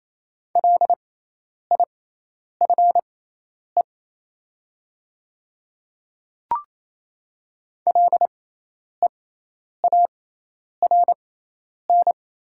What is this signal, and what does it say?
Morse code, heard as a keyed tone, about 28 wpm, spaced as for 10 wpm, 700 Hz. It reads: LIFE LEARN
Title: 6x Character Spacing - Top 300 Words - 28wpm